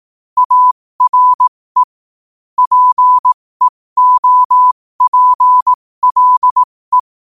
Morse code, heard as a keyed tone, about 18 wpm, spaced as for 15 wpm, 1000 Hz. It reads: ARE PEOPLE
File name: qotdfiles / 15wpm